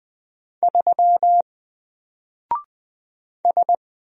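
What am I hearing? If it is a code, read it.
Morse code, heard as a keyed tone, about 20 wpm, 700 Hz. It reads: 3 S
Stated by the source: Single letter and number Morse drill at 20 wpm, from N4LCD.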